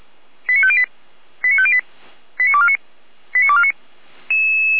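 Five-tone selective call (ZVEI), heard as quick runs of five tones, tones of about 70 ms. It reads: 89508 89508 89249 89249
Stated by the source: ZVEI five-tone call received by SDR.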